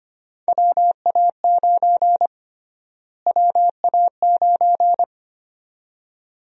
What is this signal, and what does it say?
Morse code, heard as a keyed tone, about 25 wpm, 700 Hz. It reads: WA9 WA9
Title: USA Call Sign Prefixes - Repeated 2x - 25wpm